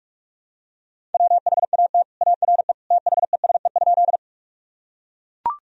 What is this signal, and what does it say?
Morse code, heard as a keyed tone, about 41 wpm, 700 Hz. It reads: WHAT ARE THESE?
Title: Sentences from Top 1000 Words - Review Part 1 - 45wpm